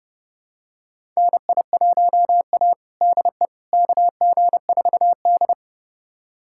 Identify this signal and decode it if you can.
Morse code, heard as a keyed tone, about 30 wpm, 700 Hz. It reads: NI1A DE KG4D
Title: USA Callsign DE Callsign - 4 Characters - 30wpm